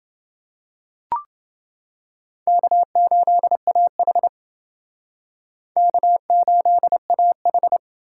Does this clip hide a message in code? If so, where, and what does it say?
Yes: Morse code from 2.5 to 7.8 s, heard as a keyed tone, about 28 wpm, 700 Hz. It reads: K8AH K8AH